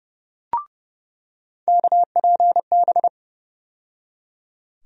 Morse code, heard as a keyed tone, about 29 wpm, 700 Hz. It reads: KPB
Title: USA Call Sign Postfixes - 3 Letters 30wpm